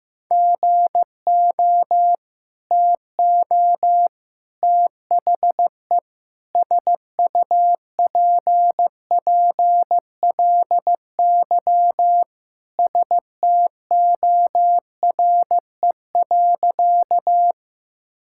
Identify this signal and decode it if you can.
Morse code, heard as a keyed tone, about 15 wpm, 700 Hz. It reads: GO TO THE SUPPLY STORE.
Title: Sentences from Top 1000 Words - Review Part 5 - 15wpm